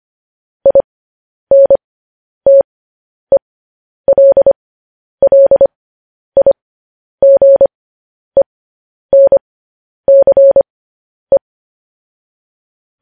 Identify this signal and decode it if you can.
Morse code, heard as a keyed tone, about 25 wpm, spaced as for 10 wpm, 550 Hz. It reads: INTELLIGENCE
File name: podcast / News headlines-1x-25x10